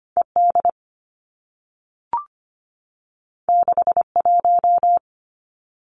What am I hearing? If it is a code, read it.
Morse code, heard as a keyed tone, about 25 wpm, 700 Hz. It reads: ED 61
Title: Two Challenging Letter-Number Combinations 25wpm